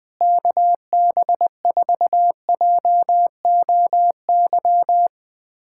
Morse code, heard as a keyed tone, about 20 wpm, 700 Hz. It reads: KB4JOY
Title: POTA QSO Practice - 20wpm